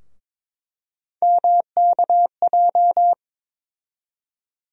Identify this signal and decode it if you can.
Morse code, heard as a keyed tone, about 21 wpm, 700 Hz. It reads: MKJ
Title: USA Call Sign Postfixes - 3 Letters 22wpm